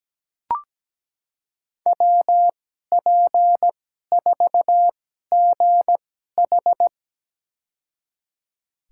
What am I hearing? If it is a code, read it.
Morse code, heard as a keyed tone, about 17 wpm, spaced as for 12 wpm, 700 Hz. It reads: WP4GH
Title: USA Call Signs - 2x Character Spacing - 17wpm